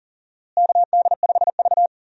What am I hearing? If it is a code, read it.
Morse code, heard as a keyed tone, about 40 wpm, 700 Hz. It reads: KD5V
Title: USA Call Signs - 4 Characters - Repeated 3x - 40wpm